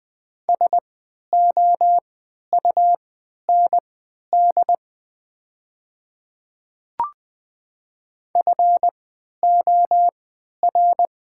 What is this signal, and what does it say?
Morse code, heard as a keyed tone, about 20 wpm, spaced as for 11 wpm, 700 Hz. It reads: SOUND FOR
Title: 3x Character Spacing - Top 100 Words - 20wpm